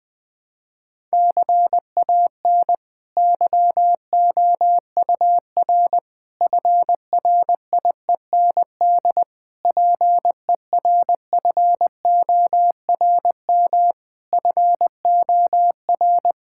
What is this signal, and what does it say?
Morse code, heard as a keyed tone, about 20 wpm, 700 Hz. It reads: CAN YOUR FRIEND PERFORM FOR
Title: Sentences from Top 1000 Words - Review Part 5 - 20wpm